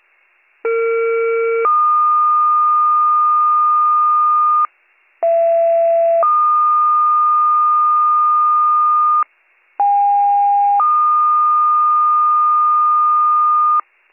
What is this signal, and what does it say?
Two-tone page, heard as a one-second tone, then a three-second tone, about 460 then 1220 Hz, about 670 then 1220 Hz, about 810 then 1220 Hz. It is heard from 0.6 to 13.8 s.